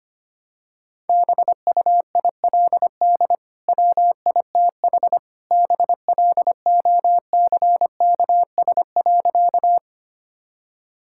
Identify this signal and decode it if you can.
Morse code, heard as a keyed tone, about 25 wpm, 700 Hz. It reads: BUILD WITH BLOCKS.
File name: Easy Sentences with Hints - 25wpm